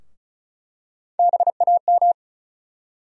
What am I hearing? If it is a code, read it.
Morse code, heard as a keyed tone, about 36 wpm, 700 Hz. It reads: BAM
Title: USA Call Sign Postfixes - 3 Letters 35wpm